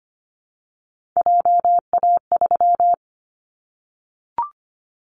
Morse code, heard as a keyed tone, about 25 wpm, 700 Hz. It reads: JA3
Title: Callsign Prefixes for Top 50 DXCC - 25wpm